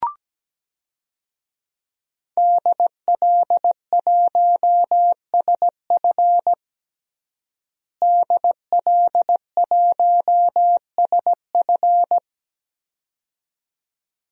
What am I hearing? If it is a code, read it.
Morse code, heard as a keyed tone, about 17 wpm, 700 Hz. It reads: DL1SF DL1SF
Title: DXCC-100 Call Signs - 5 Characters - Repeated 2x - 17wpm